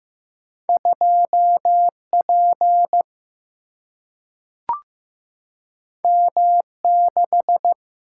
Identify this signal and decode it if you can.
Morse code, heard as a keyed tone, about 15 wpm, 700 Hz. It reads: 2 P M 6
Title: Single-Letter Single-Number 15wpm